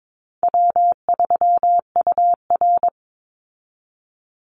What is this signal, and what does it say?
Morse code, heard as a keyed tone, about 22 wpm, 700 Hz. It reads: W3UR
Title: Call Signs Repeated 3x - 22wpm